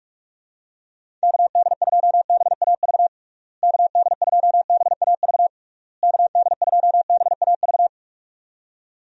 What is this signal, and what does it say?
Morse code, heard as a keyed tone, about 43 wpm, 700 Hz. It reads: KD2BAV KD2BAV KD2BAV
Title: USA Call Signs - 6 Characters - Repeated 3x - 45wpm